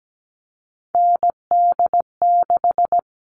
Morse code, heard as a keyed tone, about 17 wpm, 700 Hz. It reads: ND6
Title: All Cut Numbers - 3-Digit 17wpm